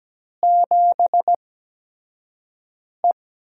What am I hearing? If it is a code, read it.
Morse code, heard as a keyed tone, about 17 wpm, spaced as for 5 wpm, 700 Hz. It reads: 7E
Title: USA Call Signs - 8x Character Spacing - 17wpm